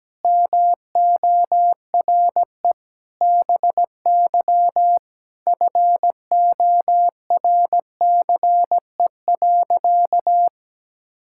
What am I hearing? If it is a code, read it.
Morse code, heard as a keyed tone, about 17 wpm, 700 Hz. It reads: MORE BY FORCE.
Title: Sentences from Top 1000 Words - Review Part 6 - 17wpm